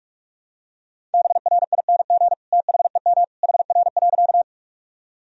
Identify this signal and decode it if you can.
Morse code, heard as a keyed tone, about 46 wpm, 700 Hz. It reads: BRING THEM HR.
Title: Sentences from Top 500 Words and Abbreviations 45wpm